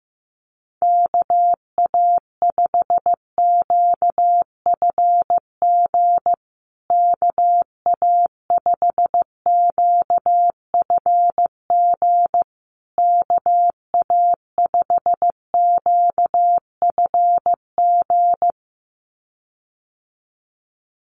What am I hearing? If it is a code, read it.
Morse code, heard as a keyed tone, about 15 wpm, 700 Hz. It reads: KA5QFG KA5QFG KA5QFG
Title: USA Call Signs Repeated 3x 15wpm